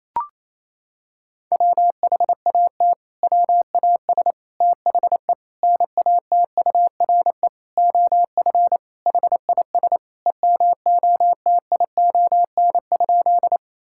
Morse code, heard as a keyed tone, about 28 wpm, 700 Hz. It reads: WHAT WAS THE NATURE OF HIS EMOTION?
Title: Sentences from Top 1600 Words 28wpm